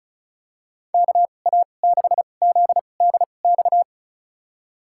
Morse code, heard as a keyed tone, about 34 wpm, spaced as for 25 wpm, 700 Hz. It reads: KA6ZDX
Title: USA Call Signs - 6 Characters - 2x Character Spacing - 35wpm